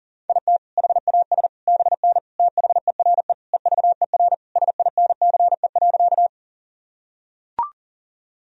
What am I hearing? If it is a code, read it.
Morse code, heard as a keyed tone, about 40 wpm, 700 Hz. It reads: IT HAS BN THERE EVER SINCE.
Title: Sentences from Top 400 Words and Abbreviations 40wpm